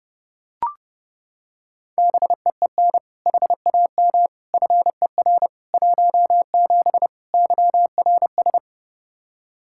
Morse code, heard as a keyed tone, about 30 wpm, 700 Hz. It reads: BEEN HAM FER 17 YRS
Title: Practice QSO - Exchange 3 - More Info - 30wpm